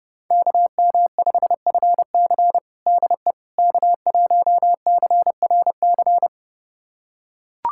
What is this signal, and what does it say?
Morse code, heard as a keyed tone, about 28 wpm, 700 Hz. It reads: KM5FC DE K1CRC
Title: USA Callsign DE Callsign - 5 Characters - 30wpm